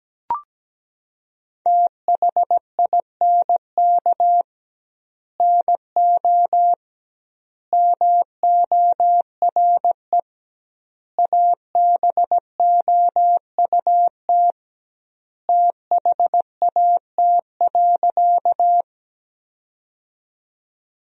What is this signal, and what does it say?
Morse code, heard as a keyed tone, about 17 wpm, 700 Hz. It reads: THINK NO MORE ABOUT THAT.
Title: 2x Word-Spacing - Sentences from Top 200 Words - 17wpm